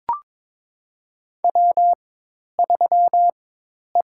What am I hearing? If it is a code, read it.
Morse code, heard as a keyed tone, about 22 wpm, spaced as for 10 wpm, 700 Hz. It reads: W3E
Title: USA Call Signs - 4x Character Spacing - 22wpm